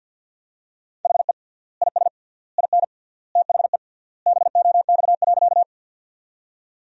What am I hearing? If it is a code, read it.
Morse code, heard as a keyed tone, about 52 wpm, 700 Hz. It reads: HE IS IN THE BOX.
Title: 3x Word-Spacing - Sentences from Top 500 Words - 50wpm